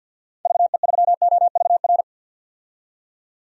Morse code, heard as a keyed tone, about 49 wpm, 700 Hz. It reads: VE3OVR
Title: DXCC-100 Call Signs - 6 Characters - 50wpm